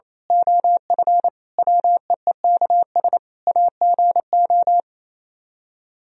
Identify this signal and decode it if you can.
Morse code, heard as a keyed tone, about 27 wpm, 700 Hz. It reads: OF WEEKS AGO
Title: Speed-Racing - Sets of 5 Words 28wpm